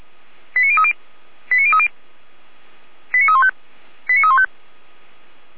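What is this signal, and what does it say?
Five-tone selective call (ZVEI), heard as quick runs of five tones, tones of about 70 ms. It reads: 89030 89030 89315 89315
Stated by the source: ZVEI five-tone call received via SDR.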